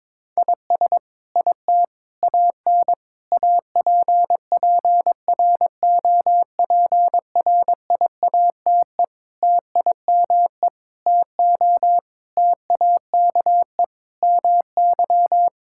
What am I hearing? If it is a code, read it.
Morse code, heard as a keyed tone, about 22 wpm, 700 Hz. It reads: IS IT AN APPROPRIATE TIME TO TAKE MY
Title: Sentences from Top 1000 Words - Review Part 5 - 22wpm